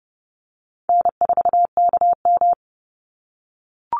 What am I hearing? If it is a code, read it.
Morse code, heard as a keyed tone, about 30 wpm, 700 Hz. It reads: N4KM
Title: Easy Call Signs - 30wpm